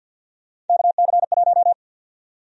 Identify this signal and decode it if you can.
Morse code, heard as a keyed tone, about 53 wpm, 700 Hz. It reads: KC1
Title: USA Call Sign Prefixes - 50wpm